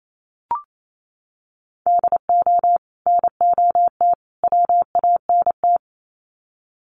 Morse code, heard as a keyed tone, about 28 wpm, 700 Hz. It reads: DO NOT WANT
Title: Sets of 3 Words 28wpm